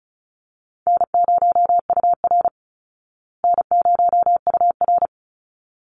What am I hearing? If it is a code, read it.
Morse code, heard as a keyed tone, about 34 wpm, 700 Hz. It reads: N0UR N0UR
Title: Easy Call Signs Repeated 2x - 35wpm